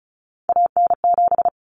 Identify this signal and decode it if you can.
Morse code, heard as a keyed tone, about 34 wpm, 700 Hz. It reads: AN7
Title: Common Cut Numbers - 3-Digit 35wpm